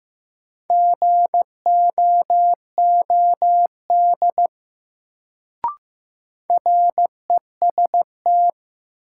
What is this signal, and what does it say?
Morse code, heard as a keyed tone, about 15 wpm, 700 Hz. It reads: GOOD REST
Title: Sets of 2 Words - New Word Drill - 15wpm